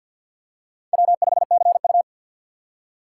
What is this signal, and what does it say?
Morse code, heard as a keyed tone, about 53 wpm, 700 Hz. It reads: W5CU
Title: USA Call Signs - 4x Character Spacing - 50wpm